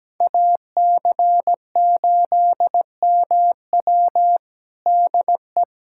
Morse code, heard as a keyed tone, about 17 wpm, 700 Hz. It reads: AC8MW DE
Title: USA Callsign DE Callsign - 5 Characters - 17wpm